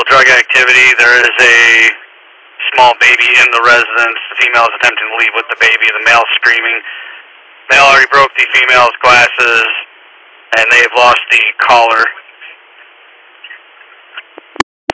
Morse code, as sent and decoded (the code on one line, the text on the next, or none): none